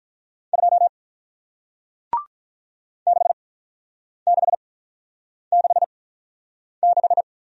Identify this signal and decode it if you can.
Morse code, heard as a keyed tone, about 48 wpm, 700 Hz. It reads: 2 6 6 6 6